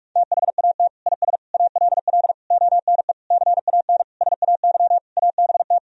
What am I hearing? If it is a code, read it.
Morse code, heard as a keyed tone, about 42 wpm, 700 Hz. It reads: THAT IS ALL ONE CAN SAY ABT
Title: Sentences from Top 200 Words and Abbreviations 45wpm